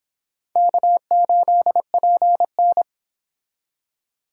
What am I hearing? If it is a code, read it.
Morse code, heard as a keyed tone, about 30 wpm, 700 Hz. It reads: K8PN